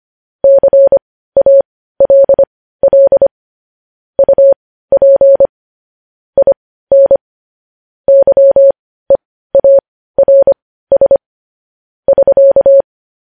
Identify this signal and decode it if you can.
Morse code, heard as a keyed tone, about 25 wpm, spaced as for 15 wpm, 550 Hz. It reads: CALL UP IN YEARS <SK>
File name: podcast / News headlines-1x-25x15